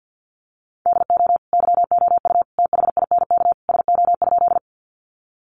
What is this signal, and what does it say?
Morse code, heard as a keyed tone, about 50 wpm, 700 Hz. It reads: DO YOU THINK SO?
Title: Sentences from Top 200 Words 50wpm